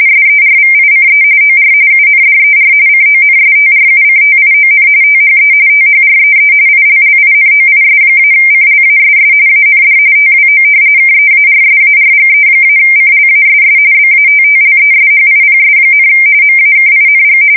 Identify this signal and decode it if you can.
RTTY, about 45.45 baud. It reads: SERVICE OR UNDER CONSTRUCTION THE THREE ILLINOIS-CLASS AND TWO KEARSARGE-CLASS BATTLESHIPS, MAKING
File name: fsk170 / battleships